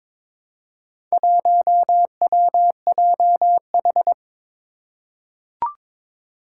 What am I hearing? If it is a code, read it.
Morse code, heard as a keyed tone, about 21 wpm, 700 Hz. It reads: 1WJH
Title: Four Challenging Letter-Number Combinations 22wpm